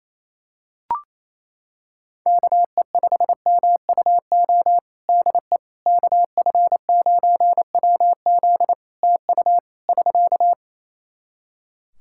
Morse code, heard as a keyed tone, about 28 wpm, 700 Hz. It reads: KE5MUO DE KF9WZ TU <SK>
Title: K1USN SST QSO Practice 28wpm